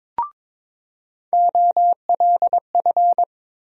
Morse code, heard as a keyed tone, about 22 wpm, 700 Hz. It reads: OLF
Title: USA Call Sign Postfixes - 3 Letters 22wpm